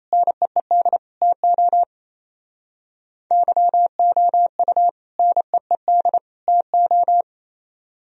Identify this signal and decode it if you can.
Morse code, heard as a keyed tone, about 35 wpm, 700 Hz. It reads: NEED TO YOU NEED TO
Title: Speed-Racing - Sets of 3 Words 28wpm